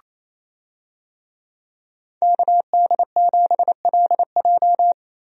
Morse code, heard as a keyed tone, about 27 wpm, 700 Hz. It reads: KD7LJ
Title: USA Call Signs - 5 Characters - Repeated 3x - 28wpm